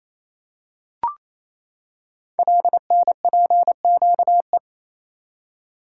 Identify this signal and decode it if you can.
Morse code, heard as a keyed tone, about 27 wpm, 700 Hz. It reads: LNPQE
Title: Code Groups - 5 Letters - 28wpm